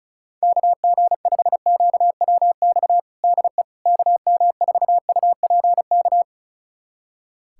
Morse code, heard as a keyed tone, about 36 wpm, 700 Hz. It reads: KG5QWX DE KM4UPK
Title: USA Callsign DE Callsign - 6 Characters - 35wpm